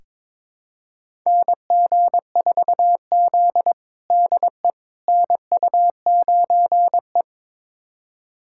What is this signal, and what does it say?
Morse code, heard as a keyed tone, about 22 wpm, 700 Hz. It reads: NG4Z DE NU9E
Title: USA Callsign DE Callsign - 4 Characters - 22wpm